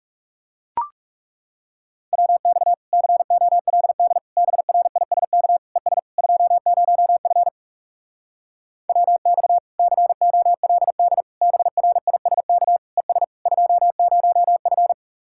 Morse code, heard as a keyed tone, about 40 wpm, 700 Hz. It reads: WX COLD BRISK ES 20F WX COLD BRISK ES 20F